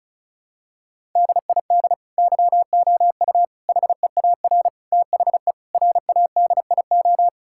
Morse code, heard as a keyed tone, about 34 wpm, 700 Hz. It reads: DID YOU HEAR THE RADIO
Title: Sentences from Top 1000 Words - Review Part 5 - 35wpm